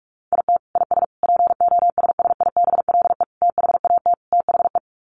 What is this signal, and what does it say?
Morse code, heard as a keyed tone, about 44 wpm, 700 Hz. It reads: IT IS POSSIBLE THAT THE
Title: Sets of 5 Words Encore2 45wpm